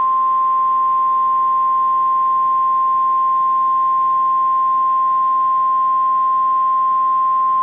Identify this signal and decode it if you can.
Emergency Alert System (SAME) tones (the 1050 Hz attention tone). It is heard from 0.6 to 7.4 s.